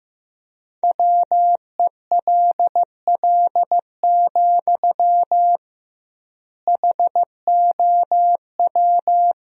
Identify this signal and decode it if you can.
Morse code, heard as a keyed tone, about 15 wpm, 700 Hz. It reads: WELL, HOW
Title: 2x Word-Spacing - Sentences from Top 200 Words - 15wpm